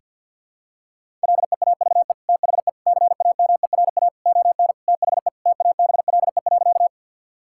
Morse code, heard as a keyed tone, about 50 wpm, 700 Hz. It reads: LEAVE THE CAMERA ON THE TABLE.
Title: Sentences from Top 1000 Words - Review Part 5 - 50wpm